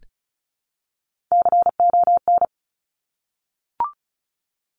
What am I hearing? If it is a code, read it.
Morse code, heard as a keyed tone, about 36 wpm, 700 Hz. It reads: CON
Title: Top Three Letter Combinations in Words 35wpm